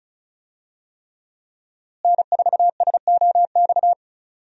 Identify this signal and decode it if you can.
Morse code, heard as a keyed tone, about 36 wpm, 700 Hz. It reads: N4SOX
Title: USA Call Signs - 5 Characters - Repeated 3x - 35wpm